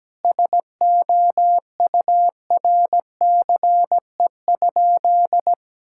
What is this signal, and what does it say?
Morse code, heard as a keyed tone, about 17 wpm, 700 Hz. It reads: SOURCE?
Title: Sentences from Top 1000 Words - Review Part 5 - 17wpm